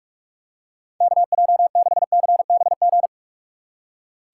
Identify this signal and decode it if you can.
Morse code, heard as a keyed tone, about 46 wpm, 700 Hz. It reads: KJ6CBG
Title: USA Call Signs - 2x Character Spacing - 45wpm